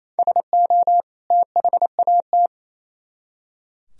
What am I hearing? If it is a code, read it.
Morse code, heard as a keyed tone, about 36 wpm, 700 Hz. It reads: SO THAT